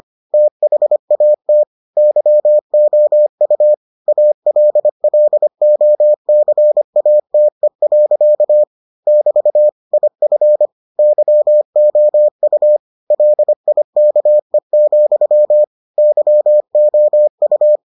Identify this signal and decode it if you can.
Morse code, heard as a keyed tone, about 25 wpm, 600 Hz. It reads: THAT YOU ALLOCATE. <BT> IF YOU LIKE, YOU